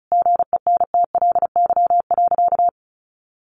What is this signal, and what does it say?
Morse code, heard as a keyed tone, about 34 wpm, 700 Hz. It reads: GENTLY.
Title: Sentences from Top 1500 Words 35wpm